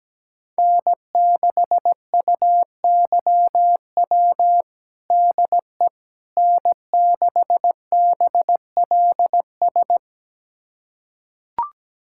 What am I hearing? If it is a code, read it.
Morse code, heard as a keyed tone, about 17 wpm, 700 Hz. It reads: N6UYW DE N6BLS
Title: USA Callsign DE Callsign - 5 Characters - 17wpm